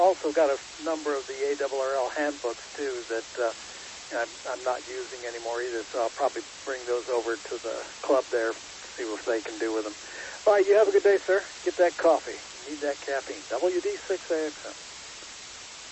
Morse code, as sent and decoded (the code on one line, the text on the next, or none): none